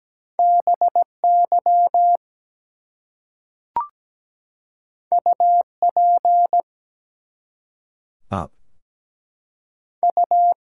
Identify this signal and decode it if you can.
Morse code, heard as a keyed tone, about 17 wpm, 700 Hz. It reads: BY UP U
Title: Two Letter Words - 17wpm